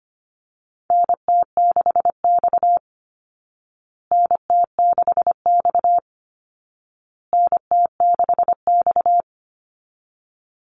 Morse code, heard as a keyed tone, about 25 wpm, 700 Hz. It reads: NT6X NT6X NT6X
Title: Call Signs Repeated 2x - 25wpm